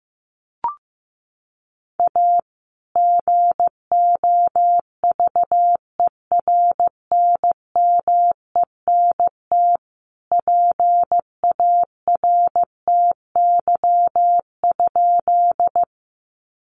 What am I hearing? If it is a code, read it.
Morse code, heard as a keyed tone, about 15 wpm, 700 Hz. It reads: A GOVERNMENT PARTY?
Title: Sentences from Top 1400 Words 15wpm